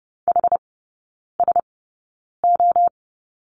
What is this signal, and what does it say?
Morse code, heard as a keyed tone, about 30 wpm, spaced as for 9 wpm, 700 Hz. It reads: HSO
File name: Lesson 19e - Letter w - Three Characters 30wpm